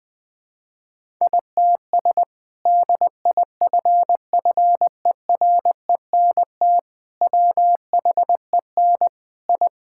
Morse code, heard as a keyed tone, about 20 wpm, 700 Hz. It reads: ITS DIFFERENT WHEN I
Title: Sentences from Top 1000 Words - Review Part 5 - 20wpm